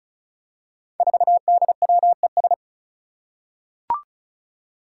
Morse code, heard as a keyed tone, about 36 wpm, 700 Hz. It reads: 4DWES
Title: Five Challenging Letter-Number Combinations 35wpm